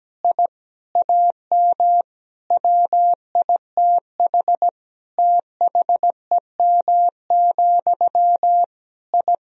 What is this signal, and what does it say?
Morse code, heard as a keyed tone, about 17 wpm, 700 Hz. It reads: I AM WITH THEM, I
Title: Sentences from Top 1000 Words - Review Part 1 - 17wpm